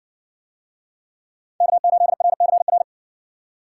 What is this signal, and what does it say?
Morse code, heard as a keyed tone, about 60 wpm, 700 Hz. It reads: K8RCR